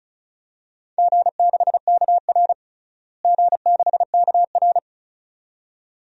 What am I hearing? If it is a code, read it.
Morse code, heard as a keyed tone, about 36 wpm, 700 Hz. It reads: G6KR G6KR
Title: DXCC-100 Call Signs - 4 Characters - Repeated 2x - 35wpm